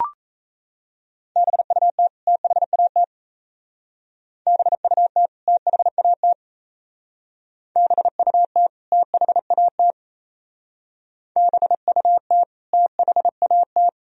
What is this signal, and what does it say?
Morse code, heard as a keyed tone, about 35 wpm, 700 Hz. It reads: BUT THAT BUT THAT BUT THAT BUT THAT